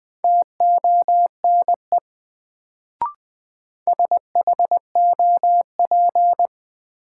Morse code, heard as a keyed tone, about 20 wpm, 700 Hz. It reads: TONE SHOP
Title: Sentences from Top 1600 Words - New Word Drill - 20wpm